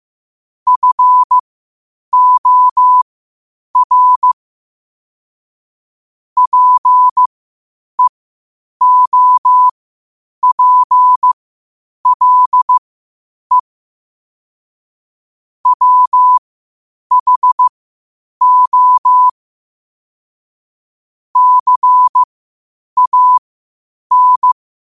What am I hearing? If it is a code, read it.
Morse code, heard as a keyed tone, about 15 wpm, spaced as for 8 wpm, 1000 Hz. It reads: FOR PEOPLE WHO CAN
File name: qotdfiles / 8wpm